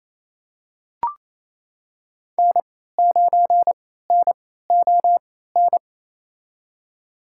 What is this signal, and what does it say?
Morse code, heard as a keyed tone, about 28 wpm, spaced as for 16 wpm, 700 Hz. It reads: N9NON
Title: USA Call Signs - 3x Character Spacing - 28wpm